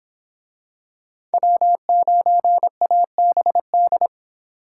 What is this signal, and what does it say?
Morse code, heard as a keyed tone, about 23 wpm, 700 Hz. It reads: W9ABD